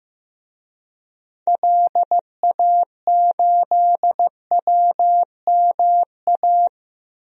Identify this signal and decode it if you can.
Morse code, heard as a keyed tone, about 15 wpm, 700 Hz. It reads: LA8WMA
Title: DXCC-100 Call Signs - 6 Characters - 15wpm